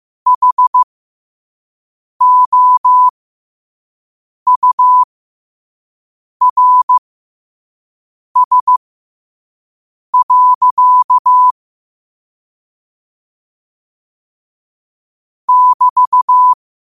Morse code, heard as a keyed tone, about 15 wpm, spaced as for 5 wpm, 1000 Hz. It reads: HOURS. <BT>
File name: qotdfiles / 5wpm